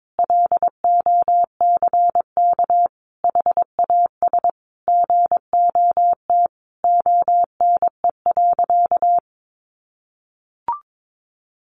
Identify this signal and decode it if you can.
Morse code, heard as a keyed tone, about 22 wpm, 700 Hz. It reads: LOCK HAS GOT ONE.
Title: Sentences from Top 1300 Words 22wpm